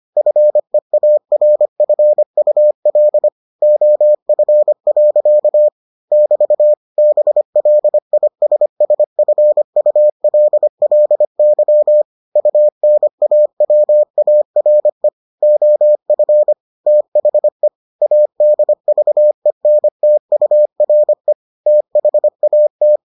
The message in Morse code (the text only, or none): FEARFUL OF. <BT> BLISSFULLY UNAWARE OF THE ADVENTURE THAT